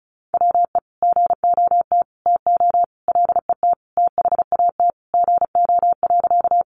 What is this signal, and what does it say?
Morse code, heard as a keyed tone, about 34 wpm, 700 Hz. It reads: WE GOT TO LET THAT GO.
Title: Sentences from Top 300 Words 35wpm